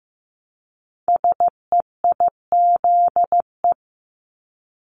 Morse code, heard as a keyed tone, about 15 wpm, 700 Hz. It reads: SEIZE
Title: Multi-letter Phonemes Repeated 3x 15wpm